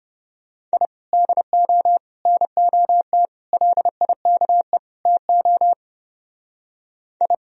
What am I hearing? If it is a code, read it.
Morse code, heard as a keyed tone, about 29 wpm, 700 Hz. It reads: I DO NOT LIKE TO I